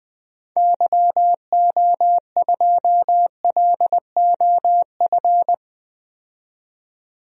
Morse code, heard as a keyed tone, about 20 wpm, 700 Hz. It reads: YO2LOF
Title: DXCC-100 Call Signs - 6 Characters - 20wpm